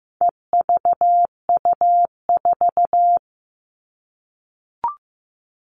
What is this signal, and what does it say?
Morse code, heard as a keyed tone, about 15 wpm, 700 Hz. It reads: EVU4
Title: All Cut Numbers - 4-Digit 15wpm